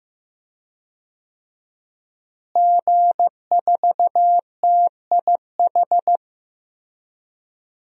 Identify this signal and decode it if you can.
Morse code, heard as a keyed tone, about 15 wpm, 700 Hz. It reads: G4TIH
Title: DXCC-100 Call Signs - 5 Characters - Repeated 2x - 15wpm